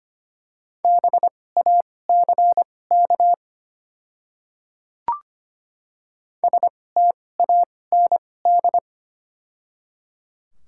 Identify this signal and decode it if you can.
Morse code, heard as a keyed tone, about 25 wpm, spaced as for 18 wpm, 700 Hz. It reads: BACK STAND
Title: 2x Character Spacing - Top 300 Words - 25wpm